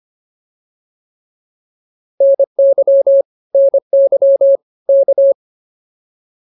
Morse code, heard as a keyed tone, about 25 wpm, 590 Hz. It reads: NY NY K